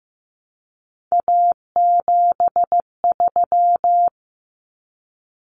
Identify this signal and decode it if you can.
Morse code, heard as a keyed tone, about 15 wpm, 700 Hz. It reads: A73
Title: Common Cut Numbers - 3-Digit 15wpm